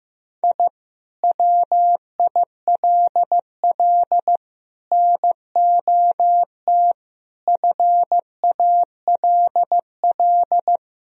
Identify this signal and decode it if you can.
Morse code, heard as a keyed tone, about 15 wpm, 700 Hz. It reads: I WILL NOT FALL
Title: Sentences from Top 1000 Words - Review Part 2 - 15wpm